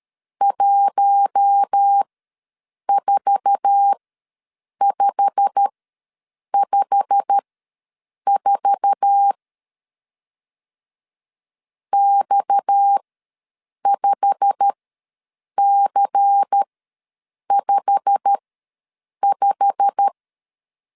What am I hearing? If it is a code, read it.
Morse code, heard as a keyed tone, about 13 wpm, spaced as for 7 wpm, 800 Hz. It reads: 14554 X5C55
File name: LEZIONI / 8 YJ45 / 8 YJ45 ws 12wpm 3146